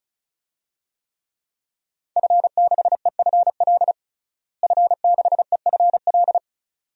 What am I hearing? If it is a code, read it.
Morse code, heard as a keyed tone, about 36 wpm, 700 Hz. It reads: F6EFL F6EFL
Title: DXCC-100 Call Signs - 5 Characters - Repeated 2x - 35wpm